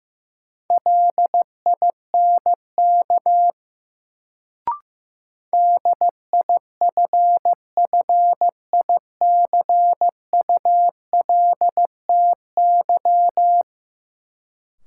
Morse code, heard as a keyed tone, about 16 wpm, 700 Hz. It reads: LINK DIFFICULTY
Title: Sentences from Top 1500 Words - New Word Drill Alt - 15wpm